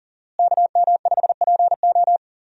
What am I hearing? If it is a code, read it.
Morse code, heard as a keyed tone, about 37 wpm, 700 Hz. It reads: KM5PO
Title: USA Call Signs - 5 Characters - Repeated 2x - 40wpm